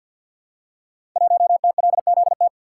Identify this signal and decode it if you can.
Morse code, heard as a keyed tone, about 48 wpm, 700 Hz. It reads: 1TLZT
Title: Code Groups - 5 Letter:Numbers - 50wpm